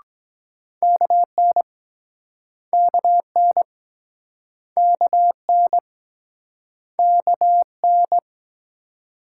Morse code, heard as a keyed tone, about 24 wpm, 700 Hz. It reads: KN KN KN KN